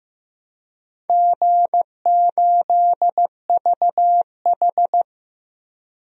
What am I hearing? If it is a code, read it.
Morse code, heard as a keyed tone, about 15 wpm, 700 Hz. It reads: G8VH
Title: DXCC-100 Call Signs - 4 Characters - 15wpm